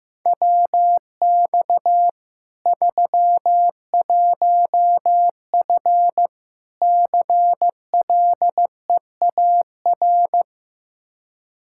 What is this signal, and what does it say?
Morse code, heard as a keyed tone, about 15 wpm, 700 Hz. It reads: WX 31F CLEAR
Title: Practice QSO - Exchange 2 - Info - 15wpm